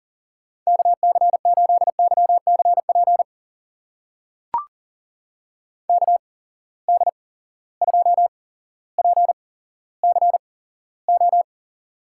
Morse code, heard as a keyed tone, about 38 wpm, spaced as for 17 wpm, 700 Hz. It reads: KC8YCP KD2PCO
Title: USA Call Signs - 6 Characters - 8x Character Spacing - 40wpm